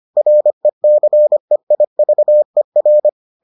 Morse code, heard as a keyed tone, about 25 wpm, 600 Hz. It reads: RECEIVER